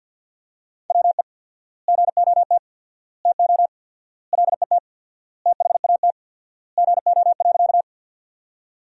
Morse code, heard as a keyed tone, about 47 wpm, 700 Hz. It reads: WE GOT TO LET THAT GO.
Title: 4x Word-Spacing - Sentences from Top 300 Words - 50wpm